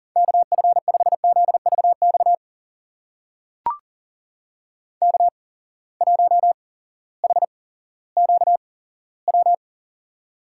Morse code, heard as a keyed tone, about 40 wpm, spaced as for 18 wpm, 700 Hz. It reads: KF5ZVX K1HQW
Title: USA Call Signs - 8x Character Spacing - 40wpm